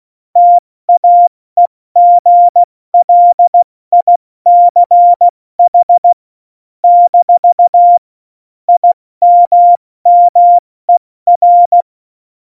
Morse code, heard as a keyed tone, about 16 wpm, 700 Hz. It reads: TAEGLICH - IMMER
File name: Morsekurs / Witze / Witz 16. April 2019, 70 F80